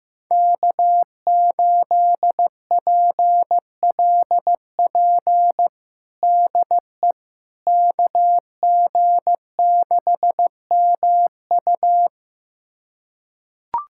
Morse code, heard as a keyed tone, about 15 wpm, 700 Hz. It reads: K8PLP DE KG6MU
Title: USA Callsign DE Callsign - 5 Characters - 15wpm